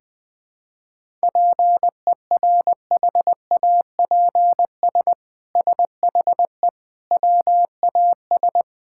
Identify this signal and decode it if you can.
Morse code, heard as a keyed tone, about 20 wpm, 700 Hz. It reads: PERHAPS SHE WAS
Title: Sentences from Top 1000 Words - Review Part 5 - 20wpm